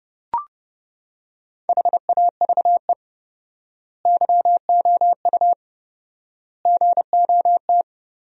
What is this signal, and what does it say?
Morse code, heard as a keyed tone, about 30 wpm, 700 Hz. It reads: HAVE YOU GOT
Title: 4x Word-Spacing - Sentences from Top 300 Words - 30wpm